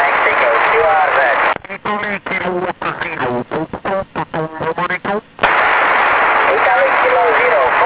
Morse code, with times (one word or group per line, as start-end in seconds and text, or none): none